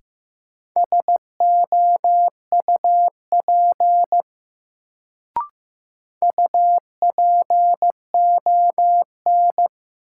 Morse code, heard as a keyed tone, about 15 wpm, 700 Hz. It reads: SOUP UPON
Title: Sentences from Top 1000 Words - Review Part 6 - New Word Drill - 15wpm